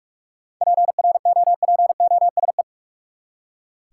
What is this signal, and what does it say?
Morse code, heard as a keyed tone, about 43 wpm, 700 Hz. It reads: PROPOSE
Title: Seven Letter Words - 45wpm